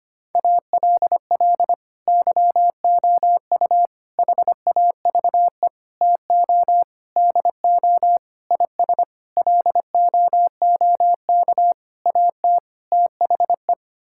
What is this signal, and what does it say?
Morse code, heard as a keyed tone, about 25 wpm, 700 Hz. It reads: ALL YOU HAVE TO DO IS LOOK AT THE
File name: Sets of 9 Words - 25wpm